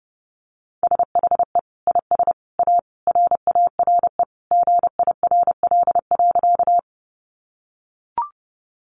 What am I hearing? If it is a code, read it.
Morse code, heard as a keyed tone, about 30 wpm, 700 Hz. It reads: SHE IS A RARE GIRL.
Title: Sentences from Top 1100 Words 30wpm